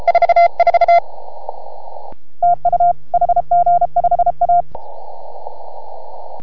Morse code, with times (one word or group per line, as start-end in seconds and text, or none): 0.1-1.0 44
2.4-2.9 TU
3.1-4.6 HG5A